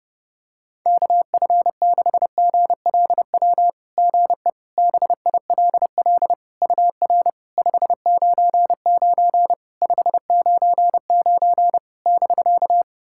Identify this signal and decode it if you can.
Morse code, heard as a keyed tone, about 30 wpm, 700 Hz. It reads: KF6GLW GE BILL UR 599 599 <BK>
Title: SOTA QSO Practice 30wpm